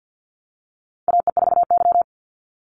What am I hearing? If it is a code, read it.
Morse code, heard as a keyed tone, about 48 wpm, 700 Hz. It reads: AE4Y
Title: NA SOTA Chaser Callsigns 50wpm